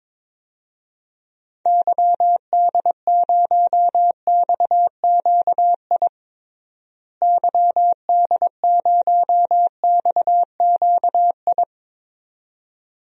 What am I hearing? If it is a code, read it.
Morse code, heard as a keyed tone, about 22 wpm, 700 Hz. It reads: YD0XQI YD0XQI
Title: DXCC-100 Call Signs - 6 Characters - Repeated 2x - 22wpm